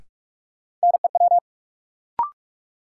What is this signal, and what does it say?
Morse code, heard as a keyed tone, about 45 wpm, 700 Hz. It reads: NEW
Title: Sentences from Top 300 Words - New Word Drill - 45wpm